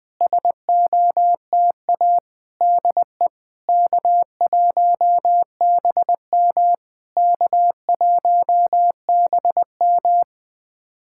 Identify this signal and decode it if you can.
Morse code, heard as a keyed tone, about 20 wpm, 700 Hz. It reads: SOTA DE K1BM K1BM
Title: SOTA QSO Practice 20wpm